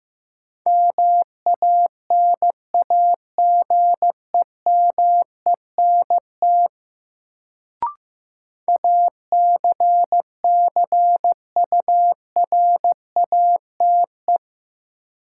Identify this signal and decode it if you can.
Morse code, heard as a keyed tone, about 15 wpm, 700 Hz. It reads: MANAGEMENT ACCURATE
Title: Sentences from Top 1200 Words - New Word Drill Alt - 15wpm